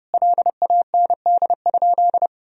Morse code, heard as a keyed tone, about 32 wpm, 700 Hz. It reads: LAND?
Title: Sentences from Top 1000 Words - Review Part 3 - 30wpm